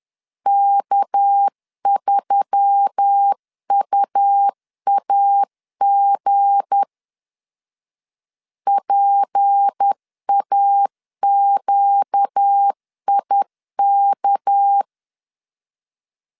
Morse code, heard as a keyed tone, about 10 wpm, 800 Hz. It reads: K3UAG PAQIK
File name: LEZIONI / 7 GF9Z / 7 G 10wpm 2501